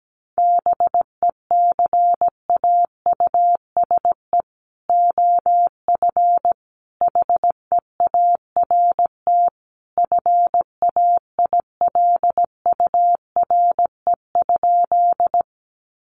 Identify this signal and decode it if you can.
Morse code, heard as a keyed tone, about 17 wpm, 700 Hz. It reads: BECAUSE OF HEART FAILURE?
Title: Sentences from Top 1200 Words 17wpm